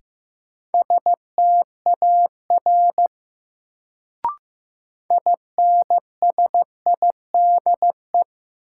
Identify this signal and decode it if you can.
Morse code, heard as a keyed tone, about 15 wpm, 700 Hz. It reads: STAR INSIDE
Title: Sentences from Top 1000 Words - Review Part 2 - New Word Drill - 15wpm